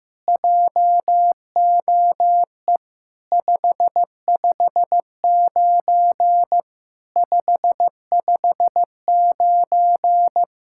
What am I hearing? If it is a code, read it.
Morse code, heard as a keyed tone, about 15 wpm, 700 Hz. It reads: JOE 559 559
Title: SOTA QSO Practice 15wpm